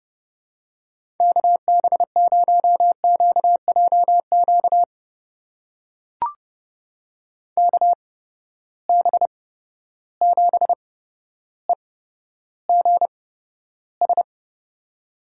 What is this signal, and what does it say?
Morse code, heard as a keyed tone, about 31 wpm, spaced as for 13 wpm, 700 Hz. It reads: KB0QJQ KB7EGS
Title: USA Call Signs - 6 Characters - 8x Character Spacing - 30wpm